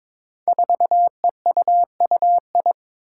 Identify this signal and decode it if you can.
Morse code, heard as a keyed tone, about 22 wpm, 700 Hz. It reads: 4EUUI
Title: Five Challenging Letter-Number Combinations 22wpm